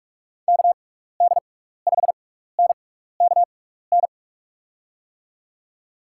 Morse code, heard as a keyed tone, about 44 wpm, spaced as for 15 wpm, 700 Hz. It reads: KD5NKN
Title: USA Call Signs - 6 Characters - 6x Character Spacing - 45wpm